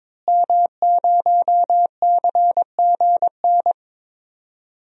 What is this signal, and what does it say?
Morse code, heard as a keyed tone, about 22 wpm, 700 Hz. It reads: M0CGN
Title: DXCC-100 Call Signs - 5 Characters - Repeated 3x - 22wpm